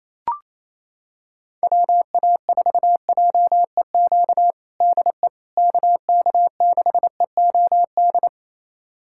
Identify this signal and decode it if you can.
Morse code, heard as a keyed tone, about 28 wpm, 700 Hz. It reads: WA4JEQ DE KK6EOD
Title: USA Callsign DE Callsign 28wpm